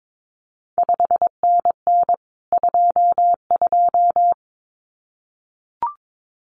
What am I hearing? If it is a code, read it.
Morse code, heard as a keyed tone, about 22 wpm, 700 Hz. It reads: 5NN 22
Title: CQ WW DX Contest Exchange Encore - 22wpm